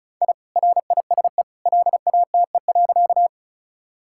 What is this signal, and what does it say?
Morse code, heard as a keyed tone, about 36 wpm, 700 Hz. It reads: I RISE LATE.
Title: Sentences from Top 1000 Words - Review Part 6 - 35wpm